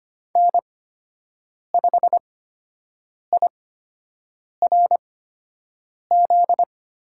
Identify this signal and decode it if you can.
Morse code, heard as a keyed tone, about 25 wpm, spaced as for 7 wpm, 700 Hz. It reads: N5IRZ
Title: USA Call Signs - 5 Characters - 8x Character Spacing - 25wpm